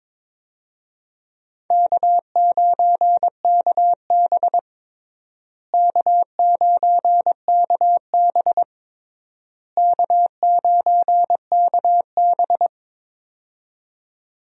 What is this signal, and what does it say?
Morse code, heard as a keyed tone, about 21 wpm, 700 Hz. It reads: K9KB K9KB K9KB
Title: USA Call Signs - 4 Characters - Repeated 3x - 22wpm